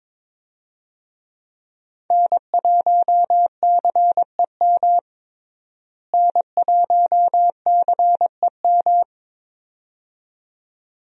Morse code, heard as a keyed tone, about 22 wpm, 700 Hz. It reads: N1CEM N1CEM
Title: USA Call Signs - 5 Characters - Repeated 2x - 22wpm